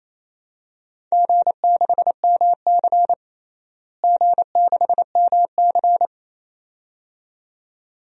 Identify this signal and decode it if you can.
Morse code, heard as a keyed tone, about 28 wpm, 700 Hz. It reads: G6MC G6MC
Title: DXCC-100 Call Signs - 4 Characters - Repeated 2x - 28wpm